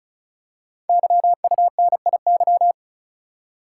Morse code, heard as a keyed tone, about 36 wpm, 690 Hz. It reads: YUNIY
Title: Code Groups - 5 Letters - 35wpm